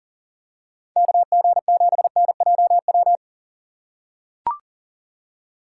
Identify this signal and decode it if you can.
Morse code, heard as a keyed tone, about 40 wpm, 700 Hz. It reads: KG7NJW
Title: USA Call Signs - 6x Character Spacing - 40wpm